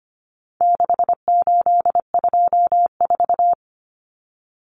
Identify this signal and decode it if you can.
Morse code, heard as a keyed tone, about 25 wpm, 700 Hz. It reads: 6824
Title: Common Cut Numbers - 4-Digit 25wpm